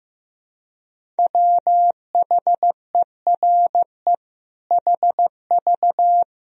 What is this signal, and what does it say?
Morse code, heard as a keyed tone, about 15 wpm, 700 Hz. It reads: WHERE HV
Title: Sentences from Top 200 Words and Abbreviations 15wpm